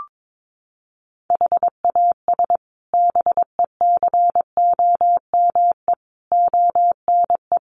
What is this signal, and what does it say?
Morse code, heard as a keyed tone, about 22 wpm, 700 Hz. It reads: HAS BECOME ONE
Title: Sets of 5 Words Encore 22wpm